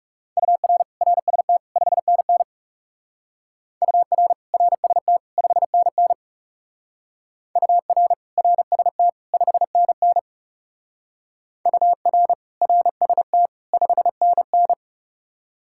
Morse code, heard as a keyed tone, about 38 wpm, 700 Hz. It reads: UR RST 5NN UR RST 5NN UR RST 5NN UR RST 5NN